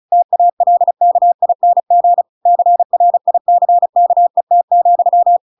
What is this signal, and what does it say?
Morse code, heard as a keyed tone, about 34 wpm, 700 Hz. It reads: TALKING CRICKET,